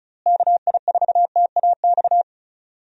Morse code, heard as a keyed tone, about 34 wpm, 700 Hz. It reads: KI4TAX
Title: USA Call Signs - 6 Characters - Repeated 3x - 35wpm